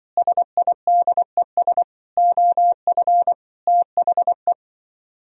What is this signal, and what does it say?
Morse code, heard as a keyed tone, about 25 wpm, 700 Hz. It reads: SIDES OF THE